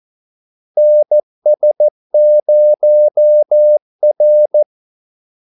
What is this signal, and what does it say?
Morse code, heard as a keyed tone, about 14 wpm, 600 Hz. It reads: NS0R
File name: Callsigns-14wpm